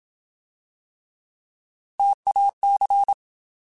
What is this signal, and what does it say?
Morse code, heard as a keyed tone, about 27 wpm, 800 Hz. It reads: TAC
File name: Alphabetical / Alphabetical mp3s / Ts @25